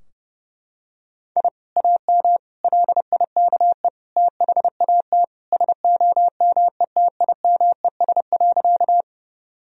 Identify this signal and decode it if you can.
Morse code, heard as a keyed tone, about 32 wpm, 700 Hz. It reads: I AM LIKE THAT SOMETIMES.
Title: Sentences from Top 1000 Words - Review Part 3 - 30wpm